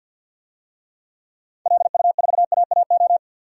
Morse code, heard as a keyed tone, about 47 wpm, 700 Hz. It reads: LU4AAO
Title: DXCC-100 Call Signs - 6 Characters - Repeated 3x - 50wpm